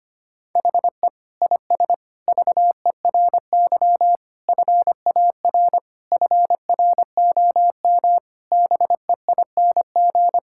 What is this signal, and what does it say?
Morse code, heard as a keyed tone, about 25 wpm, 700 Hz. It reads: HE IS VERY FAR FROM BEING
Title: Sentences from Top 1000 Words - Review Part 6 - 25wpm